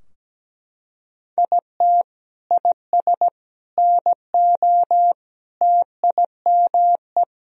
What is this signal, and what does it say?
Morse code, heard as a keyed tone, about 17 wpm, 700 Hz. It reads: IT IS NO TIME
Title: Sentences from Top 1000 Words - Review Part 3 - 17wpm